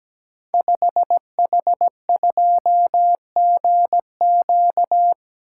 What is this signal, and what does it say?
Morse code, heard as a keyed tone, about 17 wpm, 700 Hz. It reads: 5H2GQ
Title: Code Groups - 5 Letter:Numbers - 17wpm